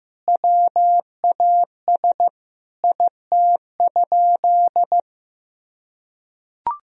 Morse code, heard as a keyed tone, about 16 wpm, 700 Hz. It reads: WAS IT?
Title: Sentences from Top 1000 Words - Review Part 1 - 15wpm